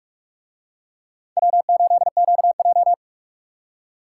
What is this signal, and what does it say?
Morse code, heard as a keyed tone, about 41 wpm, 700 Hz. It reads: W8QJ